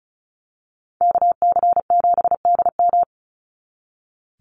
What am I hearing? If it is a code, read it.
Morse code, heard as a keyed tone, about 34 wpm, 700 Hz. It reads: KC7DM
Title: NA SOTA Activator Callsigns 35wpm